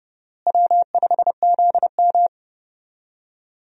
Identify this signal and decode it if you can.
Morse code, heard as a keyed tone, about 30 wpm, 700 Hz. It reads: W5ZM
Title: USA Call Signs - 4 Characters - Repeated 2x - 30wpm